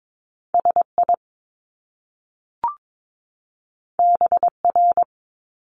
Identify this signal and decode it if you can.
Morse code, heard as a keyed tone, about 22 wpm, 700 Hz. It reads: SI BR
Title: Top Two Letter Combinations in Words 22wpm